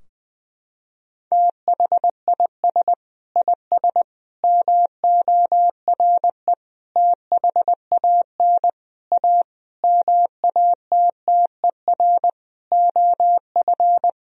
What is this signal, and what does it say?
Morse code, heard as a keyed tone, about 21 wpm, 700 Hz. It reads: THIS IS MORE THAN A MATTER OF
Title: Sentences from Top 1000 Words - Review Part 3 - 20wpm